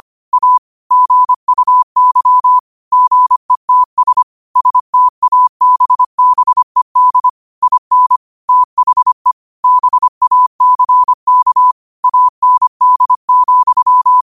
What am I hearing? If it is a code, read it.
Morse code, heard as a keyed tone, about 25 wpm, 1000 Hz. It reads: A GUY GETS STABBED IN THE BACK AND,
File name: qotdfiles / 25wpm